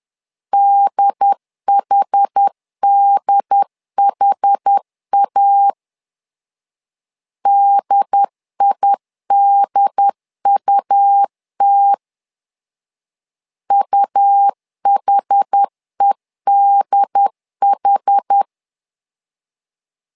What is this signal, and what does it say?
Morse code, heard as a keyed tone, about 10 wpm, 800 Hz. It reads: DHDHA DIDUT UHEDH
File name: LEZIONI / 2 UHDN / 2 UHD 10wpm 1826